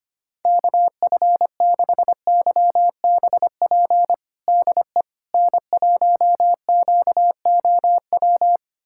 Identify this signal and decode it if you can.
Morse code, heard as a keyed tone, about 25 wpm, 700 Hz. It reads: KF6YBP DE N1QOW
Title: USA Callsign DE Callsign 25wpm